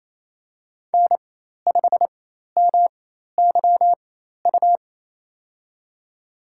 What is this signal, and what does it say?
Morse code, heard as a keyed tone, about 28 wpm, spaced as for 13 wpm, 700 Hz. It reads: N5MYU
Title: USA Call Signs - 4x Character Spacing - 28wpm